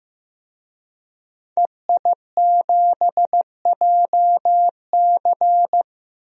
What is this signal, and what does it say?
Morse code, heard as a keyed tone, about 15 wpm, 700 Hz. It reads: EI7JC
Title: DXCC-100 Call Signs - 5 Characters - Repeated 2x - 15wpm